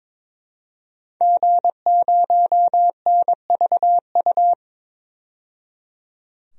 Morse code, heard as a keyed tone, about 22 wpm, 700 Hz. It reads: G0NVU
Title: DXCC-100 Call Signs - 5 Characters - 22wpm